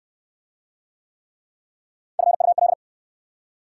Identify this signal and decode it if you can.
Morse code, heard as a keyed tone, about 64 wpm, 700 Hz. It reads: FUL